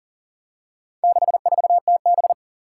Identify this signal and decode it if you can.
Morse code, heard as a keyed tone, about 40 wpm, 700 Hz. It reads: 64TB
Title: Four Challenging Letter-Number Combinations 40wpm